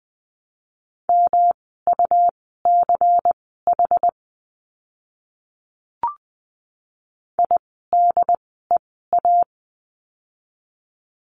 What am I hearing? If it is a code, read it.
Morse code, heard as a keyed tone, about 20 wpm, spaced as for 15 wpm, 700 Hz. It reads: MUCH IDEA
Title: 108 Lesson 18f - Letter m - Words 20wpm